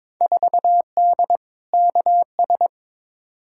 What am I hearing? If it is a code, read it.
Morse code, heard as a keyed tone, about 22 wpm, 700 Hz. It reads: 4D KS
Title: Field Day Exchanges Encore - 22wpm